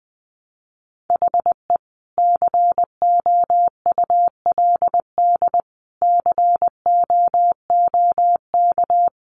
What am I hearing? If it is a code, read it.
Morse code, heard as a keyed tone, about 20 wpm, 700 Hz. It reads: HE COULD COOK
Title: Sentences from Top 700 Words 20wpm